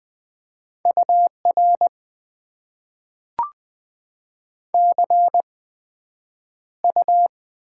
Morse code, heard as a keyed tone, about 20 wpm, 700 Hz. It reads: U R C U
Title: Top 90 QSO Elements - 8x Character Spacing - 20wpm